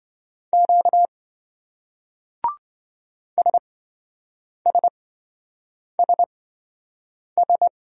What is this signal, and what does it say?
Morse code, heard as a keyed tone, about 28 wpm, spaced as for 7 wpm, 700 Hz. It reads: Q SSSS